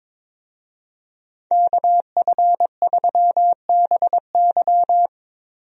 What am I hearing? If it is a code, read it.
Morse code, heard as a keyed tone, about 22 wpm, 700 Hz. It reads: KF3BY
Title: USA Call Signs - 5 Characters - Repeated 3x - 22wpm